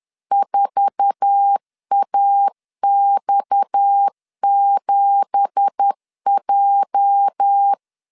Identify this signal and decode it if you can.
Morse code, heard as a keyed tone, about 10 wpm, 800 Hz. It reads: 4AX7J
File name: CW / DET / 1 9 1500C S B 10wpm 4439